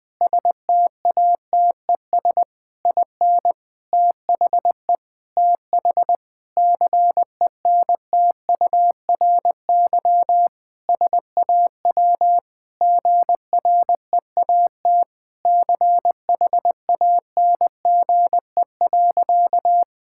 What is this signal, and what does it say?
Morse code, heard as a keyed tone, about 20 wpm, 700 Hz. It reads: STATES IN THE TH CENTURY SAW GREAT CHANGE.
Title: Sentences with 5-Word Phrases - 20wpm